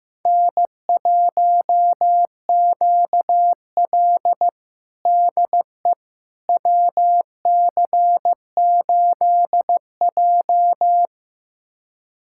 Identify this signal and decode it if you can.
Morse code, heard as a keyed tone, about 16 wpm, 700 Hz. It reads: N1QL DE WC8J
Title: USA Callsign DE Callsign - 4 Characters - 15wpm